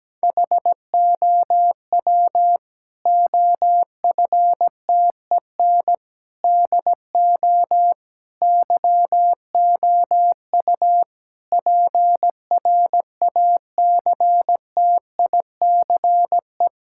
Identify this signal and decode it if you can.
Morse code, heard as a keyed tone, about 17 wpm, 700 Hz. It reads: HOW OFTEN DO YOU PRACTICE
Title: Sentences from Top 1000 Words - Review Part 6 - 17wpm